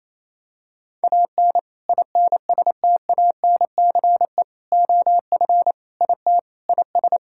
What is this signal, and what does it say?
Morse code, heard as a keyed tone, about 27 wpm, 700 Hz. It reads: AN INSTANCE OF IT IS
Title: Sentences from Top 1000 Words - Review Part 5 - 28wpm